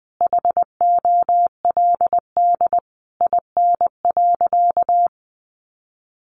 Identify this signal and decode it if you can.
Morse code, heard as a keyed tone, about 20 wpm, 700 Hz. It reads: HOLD IN.
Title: Sentences from Top 400 Words 20wpm